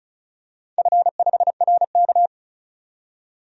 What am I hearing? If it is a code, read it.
Morse code, heard as a keyed tone, about 36 wpm, 690 Hz. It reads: F5RK
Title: DXCC-100 Call Signs - 4 Characters - 35wpm